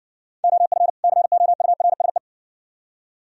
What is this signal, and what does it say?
Morse code, heard as a keyed tone, about 55 wpm, 690 Hz. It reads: OF COURSE